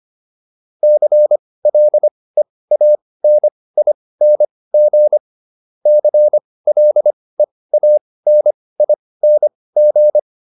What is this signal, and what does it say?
Morse code, heard as a keyed tone, about 25 wpm, spaced as for 18 wpm, 600 Hz. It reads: CLEANING CLEANING